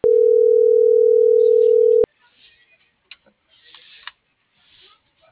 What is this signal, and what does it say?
Ringback tone (North America), 440 + 480 Hz.
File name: CENTRAL MABE / out miriam.soledad - .62077